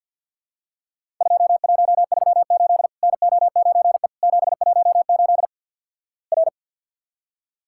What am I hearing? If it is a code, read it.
Morse code, heard as a keyed tone, about 50 wpm, 700 Hz. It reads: 2138 NO9E 718 R
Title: Worked All Europe DX Contest - QTC Exchange - 50wpm